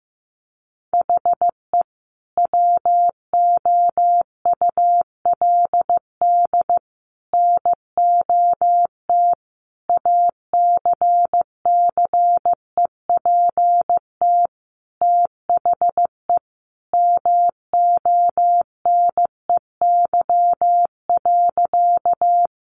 Morse code, heard as a keyed tone, about 15 wpm, 700 Hz. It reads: HE WOULD NOT ACCEPT THE MONEY.
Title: Sentences from Top 1300 Words 15wpm